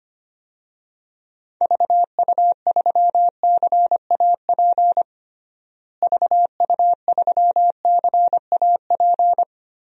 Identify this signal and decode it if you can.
Morse code, heard as a keyed tone, about 25 wpm, 700 Hz. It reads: VU3CAP VU3CAP
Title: DXCC-100 Call Signs - 6 Characters - Repeated 3x - 25wpm